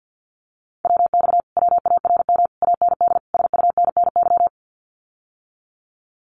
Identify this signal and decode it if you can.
Morse code, heard as a keyed tone, about 48 wpm, 700 Hz. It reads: WX WARM AND SUNNY
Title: Common QSO Phrases Encore 50wpm